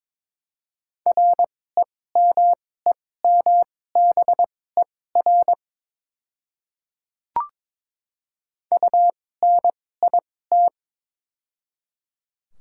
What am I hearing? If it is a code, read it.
Morse code, heard as a keyed tone, about 22 wpm, spaced as for 16 wpm, 700 Hz. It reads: REMEMBER UNIT
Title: 2x Character Spacing - Top 500 Words - 22wpm